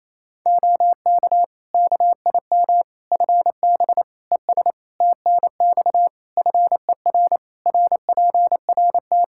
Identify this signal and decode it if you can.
Morse code, heard as a keyed tone, about 28 wpm, 700 Hz. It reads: OK KIM FB ES TNX FER RPRT
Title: Practice QSO - Exchange 2 - Info - 28wpm